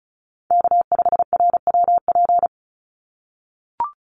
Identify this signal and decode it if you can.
Morse code, heard as a keyed tone, about 34 wpm, 700 Hz. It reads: K5RWP
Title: NA SOTA Chaser Callsigns 35wpm